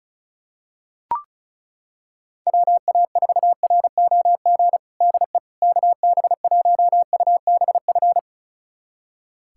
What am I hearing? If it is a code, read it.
Morse code, heard as a keyed tone, about 34 wpm, 700 Hz. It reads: WA4ROG DE KB1UBF
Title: USA Callsign DE Callsign - 6 Characters - 35wpm